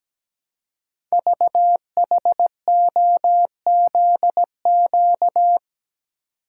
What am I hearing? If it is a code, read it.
Morse code, heard as a keyed tone, about 17 wpm, 700 Hz. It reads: VHOZQ
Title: Code Groups - 5 Letters - 17wpm